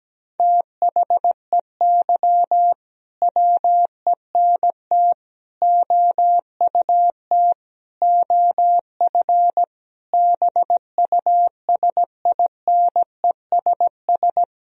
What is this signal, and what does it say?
Morse code, heard as a keyed tone, about 17 wpm, 700 Hz. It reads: THEY WENT OUT OF BUSINESS
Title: Sentences from Top 1000 Words - Review Part 4 - 17wpm